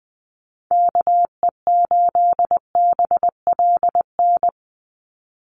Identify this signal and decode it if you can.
Morse code, heard as a keyed tone, about 20 wpm, 700 Hz. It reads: KE8BLN
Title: USA Call Signs 20wpm